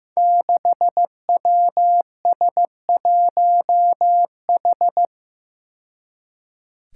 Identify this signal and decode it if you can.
Morse code, heard as a keyed tone, about 15 wpm, 700 Hz. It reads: 6WS1H
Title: Five Challenging Letter-Number Combinations 15wpm